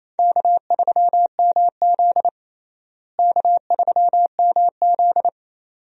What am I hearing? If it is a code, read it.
Morse code, heard as a keyed tone, about 28 wpm, 700 Hz. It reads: K3MZ K3MZ
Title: USA Call Signs - 4 Characters - Repeated 2x - 28wpm